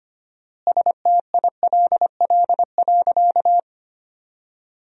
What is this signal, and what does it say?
Morse code, heard as a keyed tone, about 25 wpm, 700 Hz. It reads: STILL.
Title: 4x Word-Spacing - Sentences from Top 500 Words - 25wpm